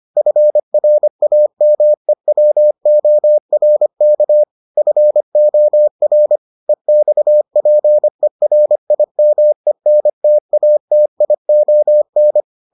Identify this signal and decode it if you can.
Morse code, heard as a keyed tone, about 25 wpm, 600 Hz. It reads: FRAMEWORK FOR EXPERIMENTATION